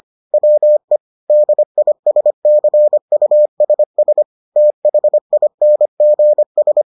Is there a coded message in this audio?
Yes: Morse code from 0.3 to 6.8 s, heard as a keyed tone, about 25 wpm, 600 Hz. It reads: WE DISCUSS THINGS